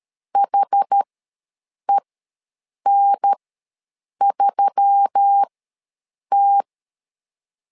Morse code, heard as a keyed tone, about 13 wpm, spaced as for 7 wpm, 800 Hz. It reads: HEN3T
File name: LEZIONI / 4 Q1S3 ws 12wpm 2038